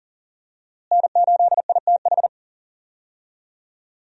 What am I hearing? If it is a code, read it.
Morse code, heard as a keyed tone, about 40 wpm, 700 Hz. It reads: N8ITH
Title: USA Call Signs - 5 Characters - Repeated 3x - 40wpm